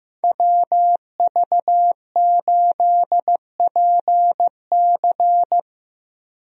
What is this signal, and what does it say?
Morse code, heard as a keyed tone, about 15 wpm, 700 Hz. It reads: WV8PC
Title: USA Call Signs - 5 Characters - Repeated 3x - 15wpm